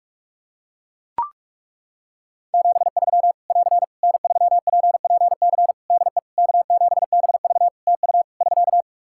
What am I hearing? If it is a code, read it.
Morse code, heard as a keyed tone, about 41 wpm, 700 Hz. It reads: 73 <AR> N3PPC DE K7BV TU <SK>
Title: Practice QSO - Exchange 4 - Ending - 45wpm